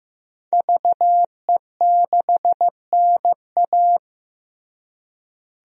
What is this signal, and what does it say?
Morse code, heard as a keyed tone, about 15 wpm, 700 Hz. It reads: VE6NA
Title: DXCC-100 Call Signs - 5 Characters - Repeated 2x - 15wpm